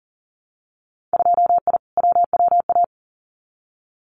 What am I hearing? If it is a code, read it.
Morse code, heard as a keyed tone, about 40 wpm, 700 Hz. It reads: 2I WWA
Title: Winter Field Day Exchanges Encore 40wpm